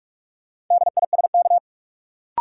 Morse code, heard as a keyed tone, about 46 wpm, 700 Hz. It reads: DISK
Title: Sentences from Top 1000 Words - Review Part 6 - New Word Drill - 45wpm